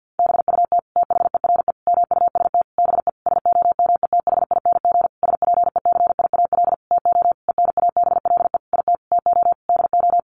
Morse code, heard as a keyed tone, about 50 wpm, 700 Hz. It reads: BUT THERE MUST BE SOMETHING SPECIAL TO ENABLE IT TO DO
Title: Sentences from Top 1500 Words 50wpm